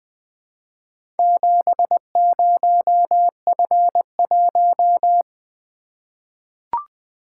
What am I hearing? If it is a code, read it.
Morse code, heard as a keyed tone, about 20 wpm, 700 Hz. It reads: 70F1
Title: Four Letter-Number Combinations - 20wpm